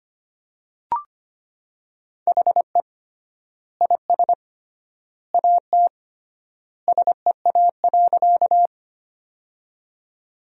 Morse code, heard as a keyed tone, about 25 wpm, 700 Hz. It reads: HE IS AT SEA.
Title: 3x Word-Spacing - Sentences from Top 300 Words - 25wpm